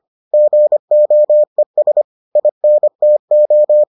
Morse code, heard as a keyed tone, about 25 wpm, 600 Hz. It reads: GOES INTO